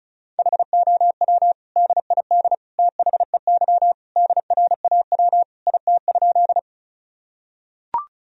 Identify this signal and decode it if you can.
Morse code, heard as a keyed tone, about 36 wpm, 700 Hz. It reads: HOW DID THEY DRAW IT?
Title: Sentences from Top 1000 Words - Review Part 1 - 35wpm